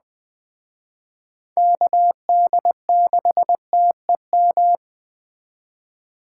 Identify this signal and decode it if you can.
Morse code, heard as a keyed tone, about 21 wpm, 700 Hz. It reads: KD6TEM